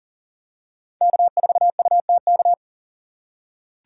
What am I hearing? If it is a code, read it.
Morse code, heard as a keyed tone, about 37 wpm, 700 Hz. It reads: K4UTK
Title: USA Call Signs - 5 Characters - 6x Character Spacing - 40wpm